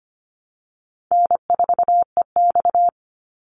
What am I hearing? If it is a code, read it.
Morse code, heard as a keyed tone, about 25 wpm, 700 Hz. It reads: N4EX
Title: NA SOTA Chaser Callsigns 25wpm